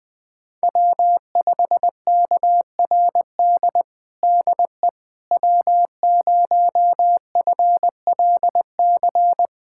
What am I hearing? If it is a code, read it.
Morse code, heard as a keyed tone, about 20 wpm, 700 Hz. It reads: W5KRD DE W0FLC
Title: USA Callsign DE Callsign - 5 Characters - 20wpm